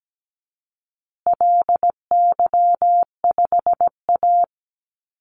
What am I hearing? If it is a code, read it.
Morse code, heard as a keyed tone, about 17 wpm, 700 Hz. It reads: LY5A
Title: Call Signs Repeated 2x - 17wpm